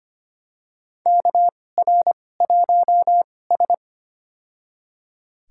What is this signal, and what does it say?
Morse code, heard as a keyed tone, about 25 wpm, spaced as for 18 wpm, 700 Hz. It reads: KR1S
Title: USA Call Signs - 2x Character Spacing - 25wpm